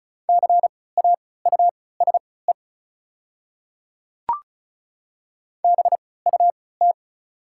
Morse code, heard as a keyed tone, about 36 wpm, spaced as for 20 wpm, 700 Hz. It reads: CAUSE BUT
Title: 3x Character Spacing - Top 200 Words - 35wpm